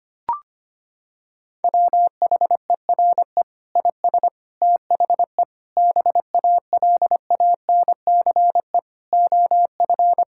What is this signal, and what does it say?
Morse code, heard as a keyed tone, about 25 wpm, 700 Hz. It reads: WHERE IS THE BALANCE OF
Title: Sentences from Top 1000 Words - Review Part 6 - 25wpm